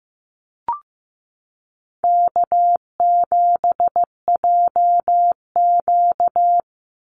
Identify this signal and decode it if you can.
Morse code, heard as a keyed tone, about 15 wpm, 700 Hz. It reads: K7JQ
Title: Easy Call Signs Repeated 2x - 15wpm